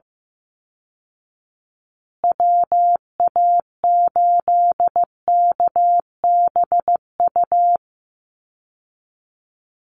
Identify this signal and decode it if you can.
Morse code, heard as a keyed tone, about 15 wpm, 700 Hz. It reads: WA8KBU
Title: Call Signs Repeated 2x - 15wpm